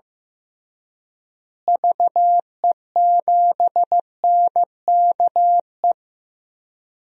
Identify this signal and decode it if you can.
Morse code, heard as a keyed tone, about 15 wpm, 700 Hz. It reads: VE7NKE
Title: DXCC-100 Call Signs - 6 Characters - Repeated 3x - 15wpm